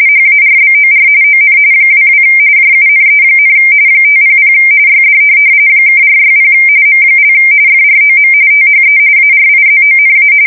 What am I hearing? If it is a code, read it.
RTTY, about 45.45 baud. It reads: HARBOR ON 22 JUNE, 500 YD (460 M) FROM THE ARIZONA
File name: fsk170 / USS-Missouri